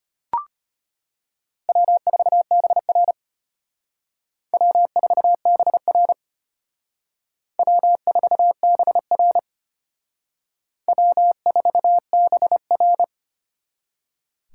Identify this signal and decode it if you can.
Morse code, heard as a keyed tone, about 32 wpm, 700 Hz. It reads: W4BR W4BR W4BR W4BR